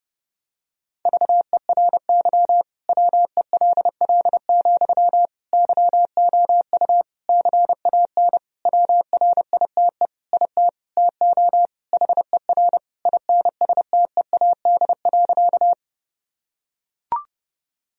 Morse code, heard as a keyed tone, about 30 wpm, 700 Hz. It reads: VERY WELL, YOU CAN WRITE IT TO HER INSTEAD.
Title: Sentences from Top 1000 Words - Review Part 4 - 30wpm